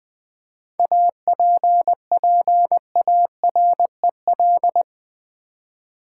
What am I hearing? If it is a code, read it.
Morse code, heard as a keyed tone, about 20 wpm, 700 Hz. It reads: APPAREL
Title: 7 Letter Words - First 3 Letters Given - 20wpm